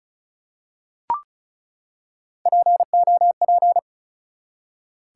Morse code, heard as a keyed tone, about 36 wpm, 700 Hz. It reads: POP
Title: Three Letter Words - 35wpm